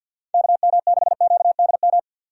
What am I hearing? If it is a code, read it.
Morse code, heard as a keyed tone, about 49 wpm, 700 Hz. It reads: KM6QDM
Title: USA Call Signs - 6x Character Spacing - 50wpm